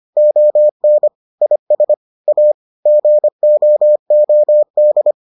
Morse code, heard as a keyed tone, about 25 wpm, 600 Hz. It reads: ON IS A GOOD